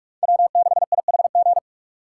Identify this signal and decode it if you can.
Morse code, heard as a keyed tone, about 44 wpm, 700 Hz. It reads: W6IHG
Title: K1USN SST QSO Practice - 45wpm